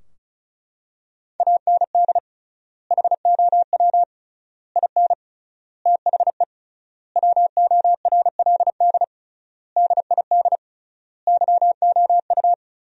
Morse code, heard as a keyed tone, about 36 wpm, 700 Hz. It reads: AND HOW IN THE WORLD DID YOU
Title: 3x Word-Spacing - Sentences from Top 200 Words - 35wpm